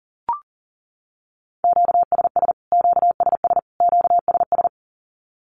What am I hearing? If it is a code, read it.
Morse code, heard as a keyed tone, about 40 wpm, 700 Hz. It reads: QSS QSS QSS
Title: Q-Codes Repeated 3x 40wpm